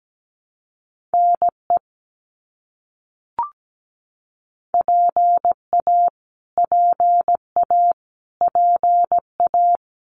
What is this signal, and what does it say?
Morse code, heard as a keyed tone, about 17 wpm, 700 Hz. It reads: NE PA PA PA
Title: US States Repeated 3x 17wpm